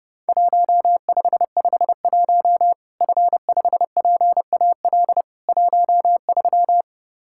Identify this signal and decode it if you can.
Morse code, heard as a keyed tone, about 30 wpm, 700 Hz. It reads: 1551 F5PAL 13
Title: Worked All Europe DX Contest - QTC Exchange - 30wpm